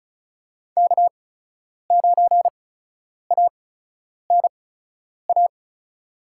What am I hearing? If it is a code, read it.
Morse code, heard as a keyed tone, about 36 wpm, spaced as for 10 wpm, 700 Hz. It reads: K9ANA
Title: USA Call Signs - 5 Characters - 8x Character Spacing - 35wpm